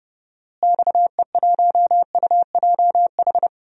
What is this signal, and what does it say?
Morse code, heard as a keyed tone, about 30 wpm, 700 Hz. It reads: XE1UJH
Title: DXCC-100 Call Signs - 6 Characters - Repeated 3x - 30wpm